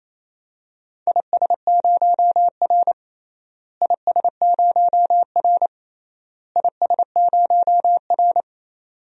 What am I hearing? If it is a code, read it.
Morse code, heard as a keyed tone, about 28 wpm, 700 Hz. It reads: IS0R IS0R IS0R
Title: DXCC-100 Call Signs - 4 Characters - Repeated 3x - 28wpm